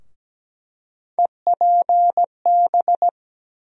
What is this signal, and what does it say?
Morse code, heard as a keyed tone, about 17 wpm, 700 Hz. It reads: EPB
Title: USA Call Sign Postfixes - 3 Letters 17wpm